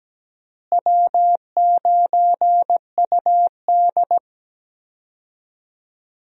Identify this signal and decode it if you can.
Morse code, heard as a keyed tone, about 17 wpm, 700 Hz. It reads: W9UD
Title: USA Call Signs - 4 Characters - Repeated 3x - 17wpm